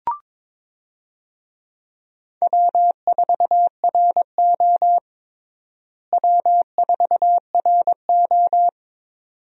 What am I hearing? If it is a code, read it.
Morse code, heard as a keyed tone, about 22 wpm, 700 Hz. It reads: W4RO W4RO
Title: USA Call Signs - 4 Characters - Repeated 2x - 22wpm